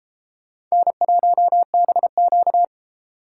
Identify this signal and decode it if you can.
Morse code, heard as a keyed tone, about 31 wpm, 700 Hz. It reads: N1BQ